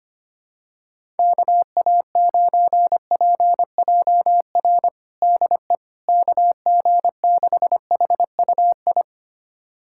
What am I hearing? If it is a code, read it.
Morse code, heard as a keyed tone, about 25 wpm, 700 Hz. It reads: KA9PJR DE KG6HUI
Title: USA Callsign DE Callsign 25wpm